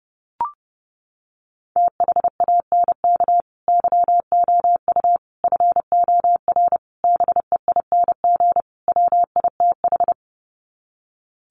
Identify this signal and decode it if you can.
Morse code, heard as a keyed tone, about 30 wpm, 700 Hz. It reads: THANK YOU FOR BEING WITH
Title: Sets of 5 Words 30wpm